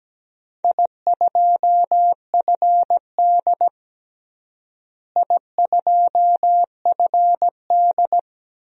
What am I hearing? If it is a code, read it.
Morse code, heard as a keyed tone, about 17 wpm, 700 Hz. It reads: I2FD I2FD
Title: DXCC-100 Call Signs - 4 Characters - Repeated 3x - 17wpm